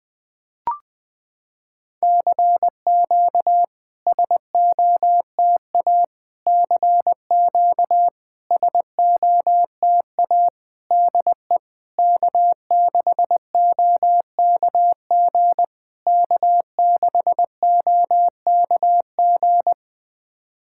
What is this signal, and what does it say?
Morse code, heard as a keyed tone, about 21 wpm, 700 Hz. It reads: CQ SOTA CQ SOTA DE K6OKG K6OKG
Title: SOTA QSO Practice 20wpm